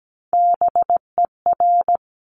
Morse code, heard as a keyed tone, about 17 wpm, 700 Hz. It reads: BER
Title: Top Three Letter Combinations in Words 17wpm